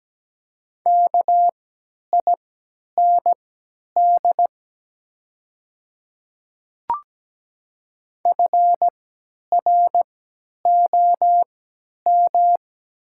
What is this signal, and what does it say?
Morse code, heard as a keyed tone, about 17 wpm, spaced as for 10 wpm, 700 Hz. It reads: KIND FROM
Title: 3x Character Spacing - Top 200 Words - 17wpm